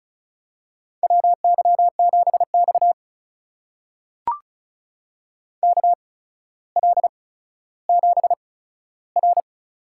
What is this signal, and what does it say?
Morse code, heard as a keyed tone, about 36 wpm, spaced as for 15 wpm, 700 Hz. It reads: WY7X KL7R
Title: USA Call Signs - 4 Characters - 8x Character Spacing - 35wpm